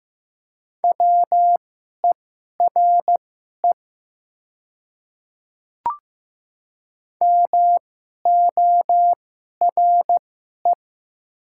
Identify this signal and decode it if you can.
Morse code, heard as a keyed tone, about 15 wpm, spaced as for 11 wpm, 700 Hz. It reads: WERE MORE
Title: 2x Character Spacing - Top 300 Words - 15wpm